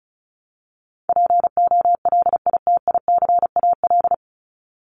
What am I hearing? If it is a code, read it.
Morse code, heard as a keyed tone, about 34 wpm, 700 Hz. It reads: POLITICAL
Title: Top 501-750 Words 35wpm